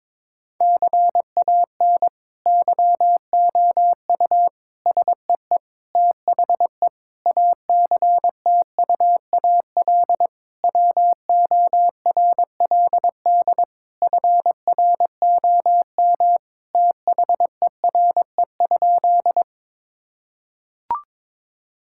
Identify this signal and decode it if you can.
Morse code, heard as a keyed tone, about 22 wpm, 700 Hz. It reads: CAN YOU SEE THE ACTUAL WORLD FROM THERE?
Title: Sentences from Top 1000 Words - Review Part 5 - 22wpm